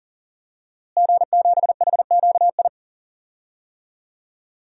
Morse code, heard as a keyed tone, about 40 wpm, 700 Hz. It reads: G7HQI
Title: DXCC-100 Call Signs - 5 Characters - Repeated 2x - 40wpm